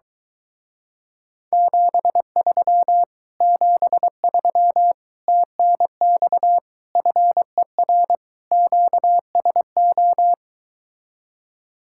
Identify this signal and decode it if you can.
Morse code, heard as a keyed tone, about 23 wpm, 700 Hz. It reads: 73 73 TNX FER QSO